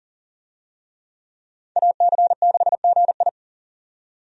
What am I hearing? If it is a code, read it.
Morse code, heard as a keyed tone, about 39 wpm, 700 Hz. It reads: AC6GI
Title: USA Call Signs - 5 Characters - 40wpm